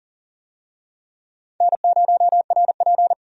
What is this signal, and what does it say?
Morse code, heard as a keyed tone, about 40 wpm, 700 Hz. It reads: N0RP
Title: USA Call Signs - 4 Characters - 40wpm